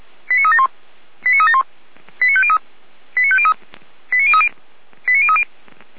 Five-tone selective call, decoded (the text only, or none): 89371 89371 89503 89503 89030 89030